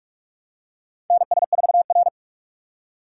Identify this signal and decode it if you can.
Morse code, heard as a keyed tone, about 46 wpm, 700 Hz. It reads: NS4R
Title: USA Call Signs - 6x Character Spacing - 45wpm